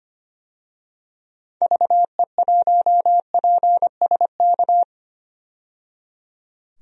Morse code, heard as a keyed tone, about 25 wpm, 700 Hz. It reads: VE1PSK
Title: DXCC-100 Call Signs - 6 Characters - 25wpm